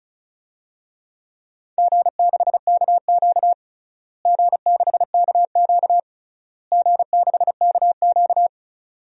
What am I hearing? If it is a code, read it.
Morse code, heard as a keyed tone, about 34 wpm, 700 Hz. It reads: G6KQ G6KQ G6KQ
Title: DXCC-100 Call Signs - 4 Characters - Repeated 3x - 35wpm